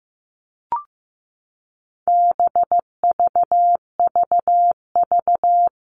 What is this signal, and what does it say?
Morse code, heard as a keyed tone, about 15 wpm, 700 Hz. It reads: BVVV
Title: All Cut Numbers - 4-Digit 15wpm